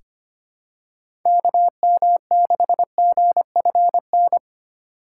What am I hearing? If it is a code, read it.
Morse code, heard as a keyed tone, about 25 wpm, 700 Hz. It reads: KM6GFN
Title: USA Call Signs - 6 Characters - 3x Character Spacing - 25wpm